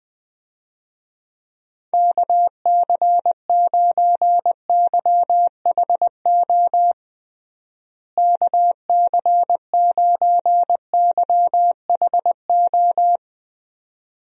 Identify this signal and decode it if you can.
Morse code, heard as a keyed tone, about 21 wpm, 700 Hz. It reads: KC9YHO KC9YHO
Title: USA Call Signs - 6 Characters - Repeated 2x - 20wpm